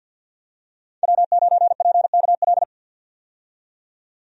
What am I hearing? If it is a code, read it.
Morse code, heard as a keyed tone, about 51 wpm, 700 Hz. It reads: W9PKL
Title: USA Call Signs - 5 Characters - 50wpm